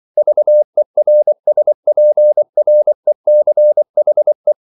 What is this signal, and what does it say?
Morse code, heard as a keyed tone, about 24 wpm, 600 Hz. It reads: VERSPRECHE